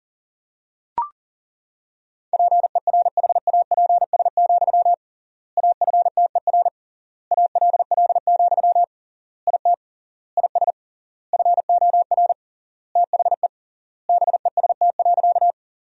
Morse code, heard as a keyed tone, about 40 wpm, 700 Hz. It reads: PERHAPS, AFTER ALL, IT IS FOR THE BEST.
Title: 3x Word-Spacing - Sentences from Top 500 Words - 40wpm